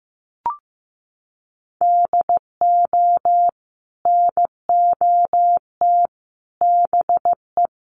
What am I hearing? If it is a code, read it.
Morse code, heard as a keyed tone, about 15 wpm, 700 Hz. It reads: DO NOT BE
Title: Sentences from Top 1500 Words 15wpm